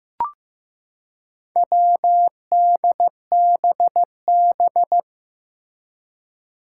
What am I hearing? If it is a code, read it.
Morse code, heard as a keyed tone, about 15 wpm, 700 Hz. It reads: WDBB
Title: Four Challenging Letter-Number Combinations 15wpm